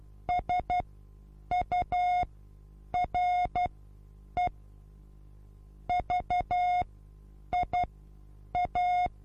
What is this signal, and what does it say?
Morse code, heard as a keyed tone, about 12 wpm, spaced as for 8 wpm, 720 Hz. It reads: SURE VIA
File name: Onlinelehrgang / morsen / DJ4UF Morsen23